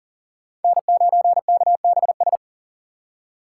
Morse code, heard as a keyed tone, about 40 wpm, 700 Hz. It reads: N9KBS
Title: USA Call Signs - 5 Characters - 3x Character Spacing - 40wpm